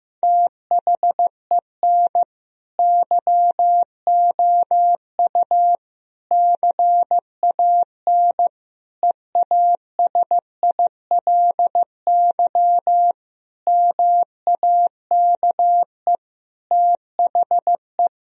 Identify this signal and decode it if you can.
Morse code, heard as a keyed tone, about 15 wpm, 700 Hz. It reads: THEN YOU CAN EASILY MAKE THE
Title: Sentences from Top 1000 Words - Review Part 1 - 15wpm